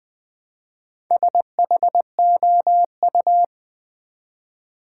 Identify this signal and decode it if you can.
Morse code, heard as a keyed tone, about 20 wpm, 700 Hz. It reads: SHOU
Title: Build A Word - 1 Letter At A Time - 5-Letter Words - 20wpm